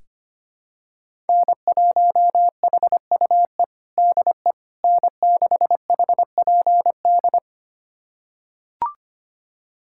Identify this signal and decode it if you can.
Morse code, heard as a keyed tone, about 25 wpm, 700 Hz. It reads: N1HUE DE N6HPD
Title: USA Callsign DE Callsign 25wpm